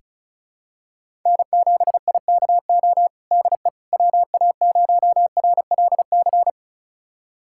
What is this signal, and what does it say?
Morse code, heard as a keyed tone, about 36 wpm, 700 Hz. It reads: N7IKO DE WA0RLC
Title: USA Callsign DE Callsign 35wpm